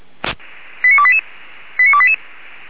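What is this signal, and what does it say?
Five-tone selective call (ZVEI), heard as quick runs of five tones, tones of about 70 ms. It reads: 89288 89288
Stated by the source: ZVEI five-tone call received with an SDR, recorded by Feuerwehr Orlamünde.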